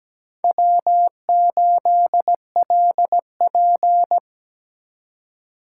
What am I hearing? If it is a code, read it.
Morse code, heard as a keyed tone, about 20 wpm, 700 Hz. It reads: W8LP
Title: Speed Racing - USA 1x2 Call Signs 17wpm